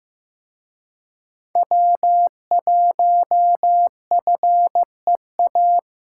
Morse code, heard as a keyed tone, about 18 wpm, 700 Hz. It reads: W1FEA
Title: Speed Racing - USA 1x3 Call Signs 15wpm